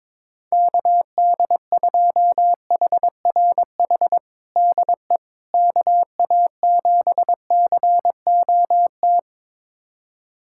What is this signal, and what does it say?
Morse code, heard as a keyed tone, about 22 wpm, 700 Hz. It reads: KD2HRH DE KA7COT
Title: USA Callsign DE Callsign 22wpm